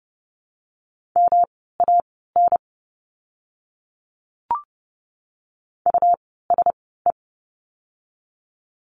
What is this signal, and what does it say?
Morse code, heard as a keyed tone, about 30 wpm, spaced as for 17 wpm, 700 Hz. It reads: MAN USE
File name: course / 30:10wpm / 237 Lesson 38f - Letter q - Words 30wpm